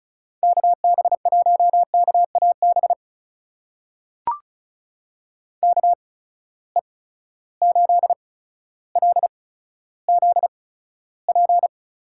Morse code, heard as a keyed tone, about 36 wpm, spaced as for 15 wpm, 700 Hz. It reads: KB1KAB KE8LZP
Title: USA Call Signs - 6 Characters - 8x Character Spacing - 35wpm